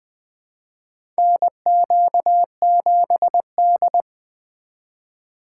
Morse code, heard as a keyed tone, about 20 wpm, 700 Hz. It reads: NQ7D
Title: USA Call Signs - 4 Characters - 4x Character Spacing - 20wpm